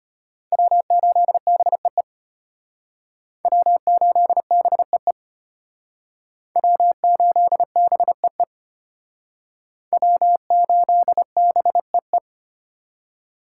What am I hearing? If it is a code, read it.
Morse code, heard as a keyed tone, about 32 wpm, 700 Hz. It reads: W8BEE W8BEE W8BEE W8BEE